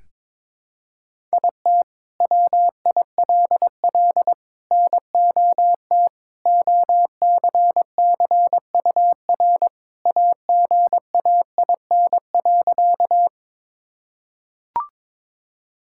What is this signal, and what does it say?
Morse code, heard as a keyed tone, about 22 wpm, 700 Hz. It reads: IT WILL NOT OCCUR AGAIN.
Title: Sentences from Top 1000 Words - Review Part 6 - 22wpm